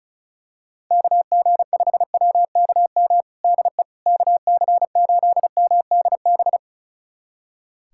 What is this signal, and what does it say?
Morse code, heard as a keyed tone, about 36 wpm, 700 Hz. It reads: KG5WKM DE KC8MDB
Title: USA Callsign DE Callsign 35wpm